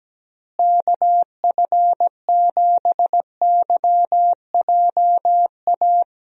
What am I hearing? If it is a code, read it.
Morse code, heard as a keyed tone, about 17 wpm, 700 Hz. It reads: KF7YJA
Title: USA Call Signs - 6 Characters - Repeated 3x - 17wpm